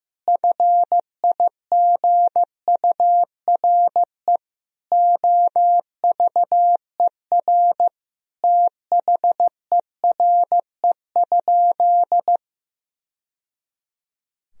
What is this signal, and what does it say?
Morse code, heard as a keyed tone, about 15 wpm, 700 Hz. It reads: FIGURE OVER THERE?
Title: Sentences from Top 1000 Words - Review Part 2 - 15wpm